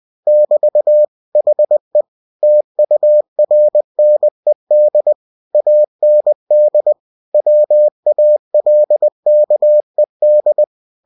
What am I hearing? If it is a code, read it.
Morse code, heard as a keyed tone, about 20 wpm, 600 Hz. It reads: <BT> HE TURNED AND WALKED
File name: cw-books / winnie-the-pooh / winnie-20wpm-page-0044